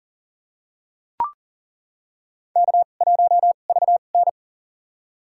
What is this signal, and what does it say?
Morse code, heard as a keyed tone, about 40 wpm, spaced as for 29 wpm, 700 Hz. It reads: K1VN
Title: USA Call Signs - 2x Character Spacing - 40wpm